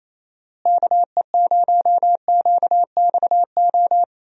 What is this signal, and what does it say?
Morse code, heard as a keyed tone, about 28 wpm, 700 Hz. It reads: KE0QXO
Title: USA Call Signs - 6 Characters - Repeated 3x - 28wpm